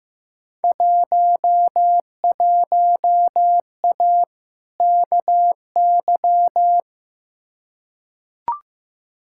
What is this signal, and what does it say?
Morse code, heard as a keyed tone, about 15 wpm, 700 Hz. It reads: 11A KY
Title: Field Day Exchanges Encore - 15wpm